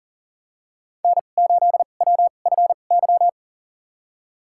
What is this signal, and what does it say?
Morse code, heard as a keyed tone, about 40 wpm, spaced as for 29 wpm, 700 Hz. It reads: N8WFY
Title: USA Call Signs - 5 Characters - 2x Character Spacing - 40wpm